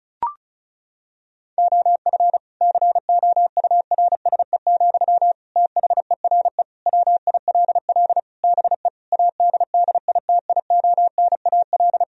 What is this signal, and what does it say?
Morse code, heard as a keyed tone, about 34 wpm, 700 Hz. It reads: OF COURSE, THERE WILL BE ADDITIONAL
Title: Sentences from Top 1000 Words - Review Part 4 - 35wpm